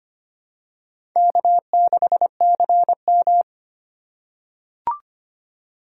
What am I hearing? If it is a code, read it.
Morse code, heard as a keyed tone, about 25 wpm, 700 Hz. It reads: K6CM
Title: USA Call Signs - 4 Characters - 4x Character Spacing - 25wpm